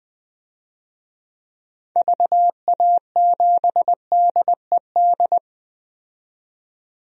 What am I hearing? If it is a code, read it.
Morse code, heard as a keyed tone, about 20 wpm, 700 Hz. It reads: VA7DED
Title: DXCC-100 Call Signs - 6 Characters - 20wpm